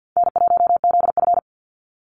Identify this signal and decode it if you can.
Morse code, heard as a keyed tone, about 50 wpm, 700 Hz. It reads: N1ZF
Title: NA SOTA Chaser Callsigns 50wpm